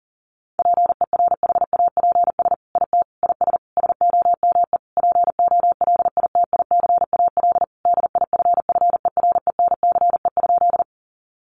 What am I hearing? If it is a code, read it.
Morse code, heard as a keyed tone, about 40 wpm, 700 Hz. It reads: PERHAPS IT IS SOME POLITICAL DIFFERENCE?
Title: Sentences from Top 800 Words 40wpm